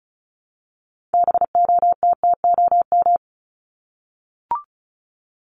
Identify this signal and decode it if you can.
Morse code, heard as a keyed tone, about 34 wpm, 700 Hz. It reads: BOTTOM
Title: Top 751-1000 Words 35wpm